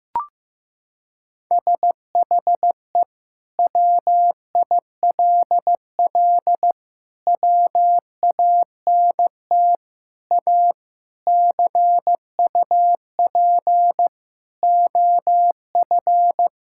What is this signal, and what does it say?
Morse code, heard as a keyed tone, about 15 wpm, 700 Hz. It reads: SHE WILL WANT A CUP OF
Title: Sentences from Top 1600 Words 15wpm